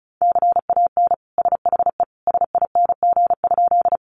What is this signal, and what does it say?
Morse code, heard as a keyed tone, about 34 wpm, 700 Hz. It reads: CAN SHE SING?
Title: Sentences from Top 400 Words 35wpm